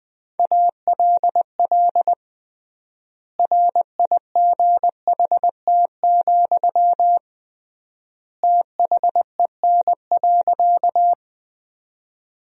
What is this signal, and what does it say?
Morse code, heard as a keyed tone, about 20 wpm, 700 Hz. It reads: ALL RIGHT, THEN.
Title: 3x Word-Spacing - Sentences from Top 200 Words - 20wpm